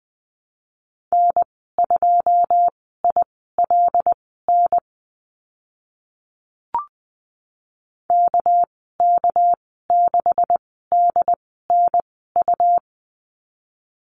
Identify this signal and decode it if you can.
Morse code, heard as a keyed tone, about 20 wpm, spaced as for 14 wpm, 700 Hz. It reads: N2ILN KK6DNU
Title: 207 Lesson 33g - Letter j - Call Signs 20wpm